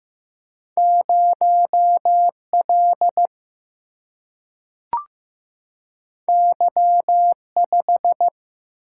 Morse code, heard as a keyed tone, about 15 wpm, 700 Hz. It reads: 0L Y5
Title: Single-Letter Single-Number 15wpm